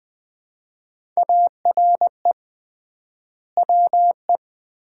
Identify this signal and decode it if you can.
Morse code, heard as a keyed tone, about 21 wpm, 700 Hz. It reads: ARE WE
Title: 3x Word-Spacing - Sets of 4 Words - 20wpm